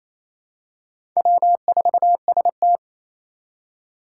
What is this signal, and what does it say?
Morse code, heard as a keyed tone, about 28 wpm, 700 Hz. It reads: W4ST
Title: Four Challenging Letter-Number Combinations 28wpm